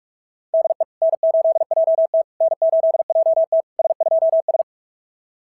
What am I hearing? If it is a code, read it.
Morse code, heard as a keyed tone, about 44 wpm, 660 Hz. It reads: DE N8JT N8JT S2S
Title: SOTA QSO Practice 45wpm